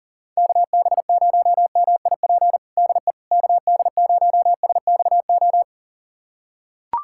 Morse code, heard as a keyed tone, about 40 wpm, 700 Hz. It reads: KB0MIP DE KD0SXO
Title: USA Callsign DE Callsign 40wpm